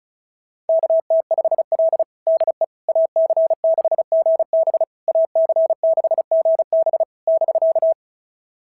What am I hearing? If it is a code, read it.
Morse code, heard as a keyed tone, about 36 wpm, 690 Hz. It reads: KT5L DE AC6GB AC6GB <BK>
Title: SOTA QSO Practice 35wpm